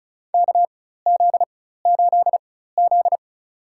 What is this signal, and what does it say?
Morse code, heard as a keyed tone, about 36 wpm, spaced as for 16 wpm, 700 Hz. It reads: KZ8Z
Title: USA Call Signs - 4 Characters - 4x Character Spacing - 35wpm